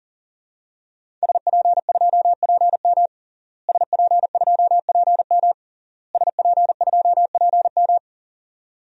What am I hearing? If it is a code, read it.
Morse code, heard as a keyed tone, about 40 wpm, 700 Hz. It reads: SP2PM SP2PM SP2PM
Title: DXCC-100 Call Signs - 5 Characters - Repeated 3x - 40wpm